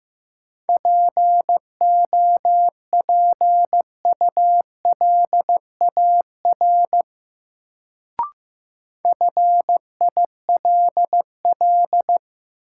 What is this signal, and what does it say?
Morse code, heard as a keyed tone, about 15 wpm, 700 Hz. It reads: POPULAR FILL
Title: Sentences from Top 1000 Words - Review Part 3 - New Word Drill - 15wpm